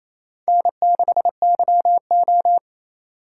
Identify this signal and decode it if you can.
Morse code, heard as a keyed tone, about 27 wpm, 700 Hz. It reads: N6YO
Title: USA Call Signs - 4 Characters - 3x Character Spacing - 28wpm